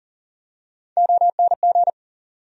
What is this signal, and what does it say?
Morse code, heard as a keyed tone, about 40 wpm, 700 Hz. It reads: ONG
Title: USA Call Sign Postfixes - 3 Letters 40wpm